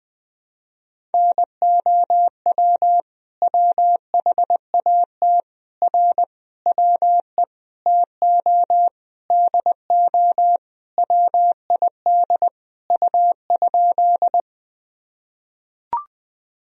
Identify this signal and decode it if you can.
Morse code, heard as a keyed tone, about 21 wpm, 700 Hz. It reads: NOW WHAT R WE TO DO WID U?
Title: Sentences from Top 100 Words and Abbreviations 20wpm